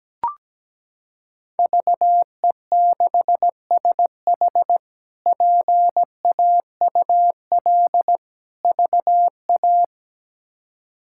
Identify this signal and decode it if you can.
Morse code, heard as a keyed tone, about 17 wpm, 700 Hz. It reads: VE6SH PAUL VA
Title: K1USN SST QSO Practice - 17wpm